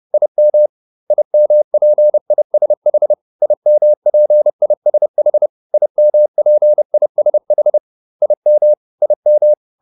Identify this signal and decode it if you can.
Morse code, heard as a keyed tone, about 30 wpm, 600 Hz. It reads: IM IMPISH IMPISH IMPISH IM IM
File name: IM6 30